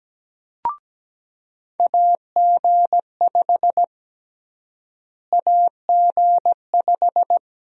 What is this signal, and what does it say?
Morse code, heard as a keyed tone, about 17 wpm, 700 Hz. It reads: AG5 AG5
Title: USA Call Sign Prefixes - Repeated 2x - 17wpm